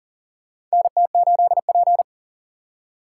Morse code, heard as a keyed tone, about 40 wpm, 700 Hz. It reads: NT8P
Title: USA Call Signs - 4 Characters - 4x Character Spacing - 40wpm